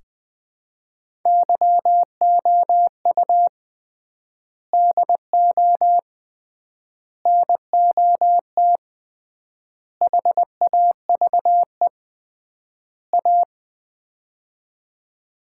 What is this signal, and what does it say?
Morse code, heard as a keyed tone, about 20 wpm, 700 Hz. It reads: YOU DO NOT HAVE A
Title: 3x Word-Spacing - Sets of 5 Words - 20wpm